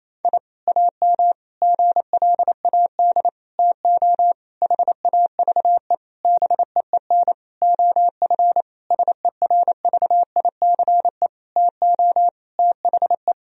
Morse code, heard as a keyed tone, about 28 wpm, 700 Hz. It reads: I AM GLAD TO HAVE BEEN OF SERVICE TO THE
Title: Sentences from Top 1000 Words - Review Part 4 - 28wpm